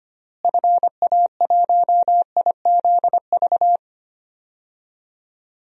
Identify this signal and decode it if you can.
Morse code, heard as a keyed tone, about 25 wpm, 700 Hz. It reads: FA1IZV
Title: DXCC-100 Call Signs - 6 Characters - Repeated 3x - 25wpm